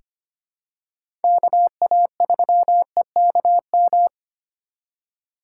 Morse code, heard as a keyed tone, about 25 wpm, 700 Hz. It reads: KA3EKM
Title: USA Call Signs - 6 Characters - 2x Character Spacing - 25wpm